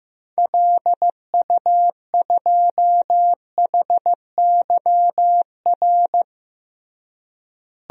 Morse code, heard as a keyed tone, about 15 wpm, 700 Hz. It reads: LU2HYR
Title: DXCC-100 Call Signs - 6 Characters - Repeated 3x - 15wpm